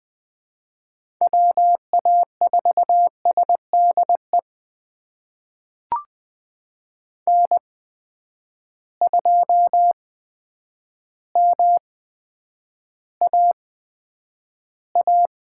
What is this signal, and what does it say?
Morse code, heard as a keyed tone, about 20 wpm, spaced as for 9 wpm, 700 Hz. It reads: WA4SDE N2MAA
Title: USA Call Signs - 8x Character Spacing - 20wpm